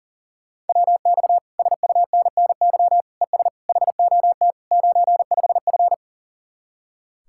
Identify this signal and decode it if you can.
Morse code, heard as a keyed tone, about 52 wpm, 700 Hz. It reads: WX SUNNY ES HOT 95F